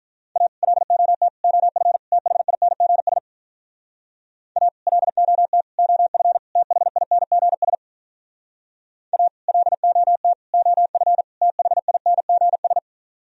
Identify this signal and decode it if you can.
Morse code, heard as a keyed tone, about 45 wpm, 700 Hz. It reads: A LOT OF THINGS A LOT OF THINGS A LOT OF THINGS